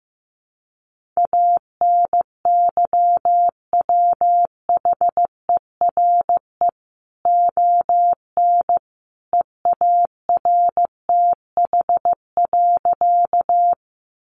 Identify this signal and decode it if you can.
Morse code, heard as a keyed tone, about 15 wpm, 700 Hz. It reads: ANYWHERE ON EARTH.
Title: Sentences from Top 1400 Words 15wpm